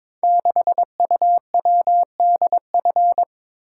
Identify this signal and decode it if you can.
Morse code, heard as a keyed tone, about 23 wpm, 690 Hz. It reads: 6UWDF
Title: Code Groups - 5 Letter:Numbers - 22wpm